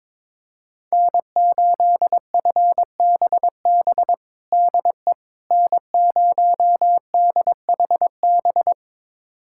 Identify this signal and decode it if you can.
Morse code, heard as a keyed tone, about 22 wpm, 700 Hz. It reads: N8FBB DE N0DHB
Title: USA Callsign DE Callsign - 5 Characters - 22wpm